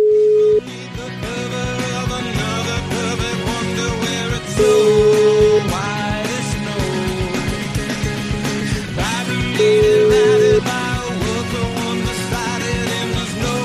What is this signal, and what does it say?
Ringback tone (Europe, 425 Hz), with noise.